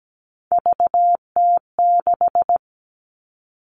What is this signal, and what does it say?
Morse code, heard as a keyed tone, about 17 wpm, 700 Hz. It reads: VT6
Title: All Cut Numbers - 3-Digit 17wpm